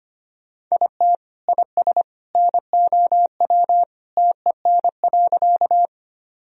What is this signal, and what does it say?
Morse code, heard as a keyed tone, about 25 wpm, 700 Hz. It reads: IT IS NOW TEN.
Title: Sentences from Top 1000 Words - Review Part 2 - 25wpm